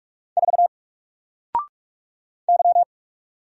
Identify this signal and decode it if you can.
Morse code, heard as a keyed tone, about 46 wpm, 700 Hz. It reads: 4 Y
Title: Single Letter-Number 45wpm